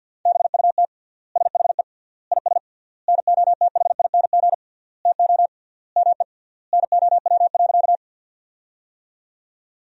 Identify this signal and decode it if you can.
Morse code, heard as a keyed tone, about 53 wpm, 700 Hz. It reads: BUT SHE IS NOTHING TO ME NOW.
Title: 3x Word-Spacing - Sentences from Top 500 Words - 50wpm